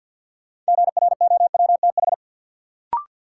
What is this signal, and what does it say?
Morse code, heard as a keyed tone, about 46 wpm, 700 Hz. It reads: GROWTH
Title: Sentences from Top 1000 Words - Review Part 5 - New Word Drill - 50wpm